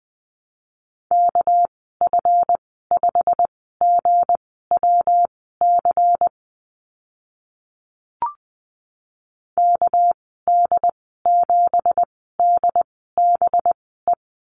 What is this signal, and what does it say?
Morse code, heard as a keyed tone, about 20 wpm, spaced as for 14 wpm, 700 Hz. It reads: KF5GWC KD7DBE
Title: 200 Lesson 32g - Letter k - Call Signs 20wpm